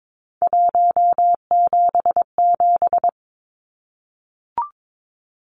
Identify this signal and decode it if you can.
Morse code, heard as a keyed tone, about 22 wpm, 700 Hz. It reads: 177
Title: Numbers - 3-Digit 22wpm